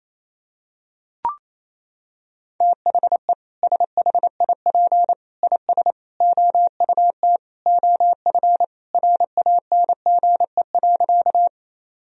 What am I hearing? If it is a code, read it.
Morse code, heard as a keyed tone, about 28 wpm, 700 Hz. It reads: THE SHIP IS OUT OF RANGE.
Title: Sentences from Top 1000 Words - Review Part 4 - 28wpm